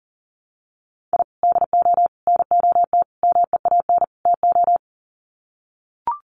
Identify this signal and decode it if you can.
Morse code, heard as a keyed tone, about 40 wpm, 700 Hz. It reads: I DO NOT MEAN TO
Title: Sets of 5 Words 40wpm